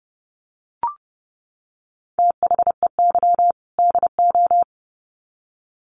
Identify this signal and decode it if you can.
Morse code, heard as a keyed tone, about 30 wpm, 700 Hz. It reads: THEY DO
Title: Sets of 2 Words 30wpm